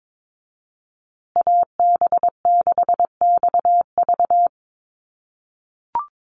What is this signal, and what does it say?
Morse code, heard as a keyed tone, about 22 wpm, 700 Hz. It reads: AB6XV
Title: USA Call Signs Repeated 2x 22wpm